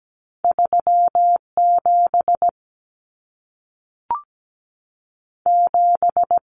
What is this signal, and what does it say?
Morse code, heard as a keyed tone, about 17 wpm, 700 Hz. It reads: 37 7
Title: Numbers - 2-Digit 17wpm